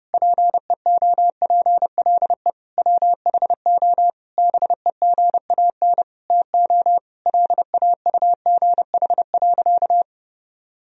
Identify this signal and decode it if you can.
Morse code, heard as a keyed tone, about 32 wpm, 700 Hz. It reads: PEOPLE WHO BEGAN TO LAUGH.
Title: Sentences from Top 1000 Words - Review Part 1 - 30wpm